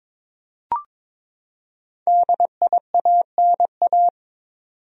Morse code, heard as a keyed tone, about 22 wpm, 700 Hz. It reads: DIANA
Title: USA Common Names - 22wpm